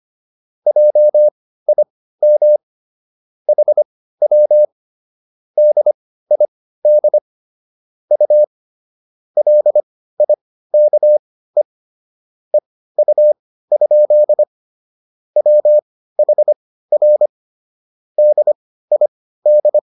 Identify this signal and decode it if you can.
Morse code, heard as a keyed tone, about 25 wpm, spaced as for 15 wpm, 600 Hz. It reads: JIM HW DID U LIKE EU? WHR DID